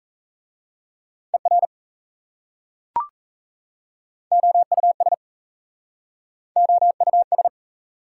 Morse code, heard as a keyed tone, about 37 wpm, 690 Hz. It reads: ER OUS OUS